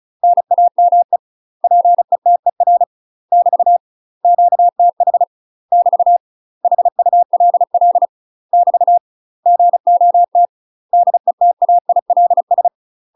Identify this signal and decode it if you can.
Morse code, heard as a keyed tone, about 34 wpm, 700 Hz. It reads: NAME PETER <BT> QTH <BT> HULL <BT> GOT DETAILS
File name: UK MASTER QSOs TRAINING / 35WPM / UK MASTER QSOs 0043